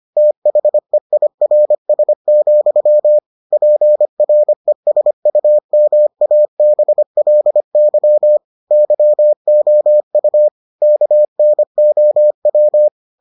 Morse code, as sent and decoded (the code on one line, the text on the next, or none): - .... .. .-. ... --..-- / .--. .-. ... ..- -- .- -... .-.. -.-- / -.-- --- ..- / -.- -. --- .--
THEIRS, PRESUMABLY YOU KNOW